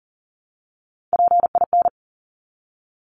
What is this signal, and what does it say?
Morse code, heard as a keyed tone, about 40 wpm, 700 Hz. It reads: PIN
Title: Top 1751-2000 Words 40wpm